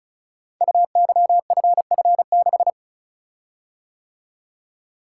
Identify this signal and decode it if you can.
Morse code, heard as a keyed tone, about 36 wpm, 700 Hz. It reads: UYFF6
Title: Code Groups - 5 Letter:Numbers - 35wpm